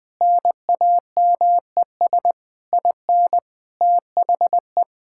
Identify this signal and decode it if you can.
Morse code, heard as a keyed tone, about 21 wpm, 700 Hz. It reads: NAMES IN THE
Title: Sentences with 7-Word Phrases - 20wpm